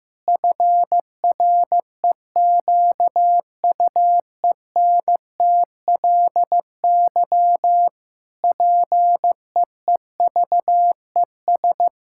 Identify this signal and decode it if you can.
Morse code, heard as a keyed tone, about 15 wpm, 700 Hz. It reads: FREQUENTLY PEEVES
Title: Pangram Challenge v2 - 15wpm